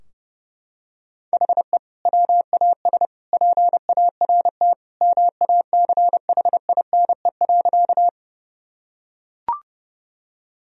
Morse code, heard as a keyed tone, about 30 wpm, 700 Hz. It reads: HE WAS PART MACHINE.
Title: Sentences from Top 1000 Words - Review Part 2 - 30wpm